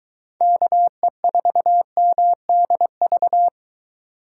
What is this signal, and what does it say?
Morse code, heard as a keyed tone, about 22 wpm, 700 Hz. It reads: KE4MDV